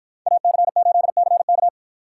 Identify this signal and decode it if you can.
Morse code, heard as a keyed tone, about 53 wpm, 700 Hz. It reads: AC8CK